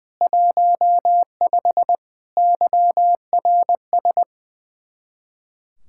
Morse code, heard as a keyed tone, about 20 wpm, 700 Hz. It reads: 15 YRS
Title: Speed-Racing Common QSO Phrases - 20wpm